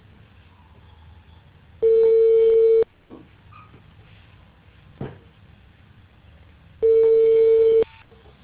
Ringback tone (China, 450 Hz).